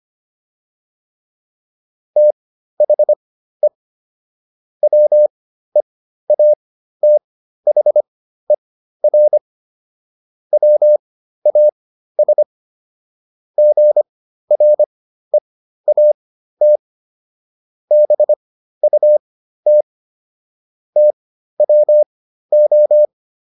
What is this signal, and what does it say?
Morse code, heard as a keyed tone, about 25 wpm, spaced as for 13 wpm, 600 Hz. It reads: THE WEATHER WAS GREAT BUT TWO